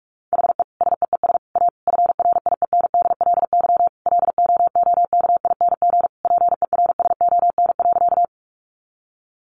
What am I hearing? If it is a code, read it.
Morse code, heard as a keyed tone, about 44 wpm, 700 Hz. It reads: HE SEES A FRIENDLY LOOKING PERSON.
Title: Easy Phrases 45wpm